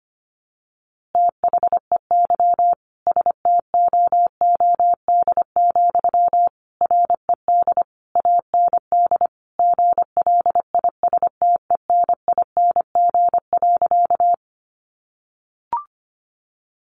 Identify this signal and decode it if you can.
Morse code, heard as a keyed tone, about 25 wpm, 700 Hz. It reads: THEY STOOD, RED AND GLISTENING.
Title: A Sound Of Thunder 25wpm